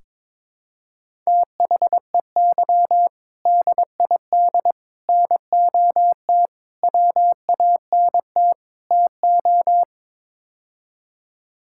Morse code, heard as a keyed tone, about 22 wpm, 700 Hz. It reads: THEY DID NOT WANT TO
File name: practice / 22wpm / Build A Phrase - 5-Word Phrases - 22wpm